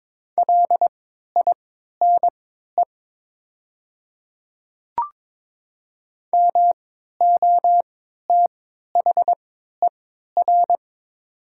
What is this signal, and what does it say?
Morse code, heard as a keyed tone, about 22 wpm, spaced as for 12 wpm, 700 Hz. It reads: LINE MOTHER
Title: 3x Character Spacing - Top 200 Words - 22wpm